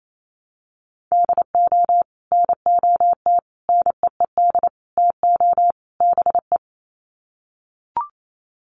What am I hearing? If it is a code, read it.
Morse code, heard as a keyed tone, about 28 wpm, 700 Hz. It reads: DO NOT NEED TO BE
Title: Sets of 5 Words Encore 28wpm